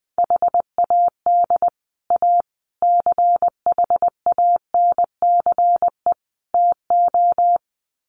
Morse code, heard as a keyed tone, about 20 wpm, 700 Hz. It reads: HAD A CHANCE TO
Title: Sets of 4 Words 20wpm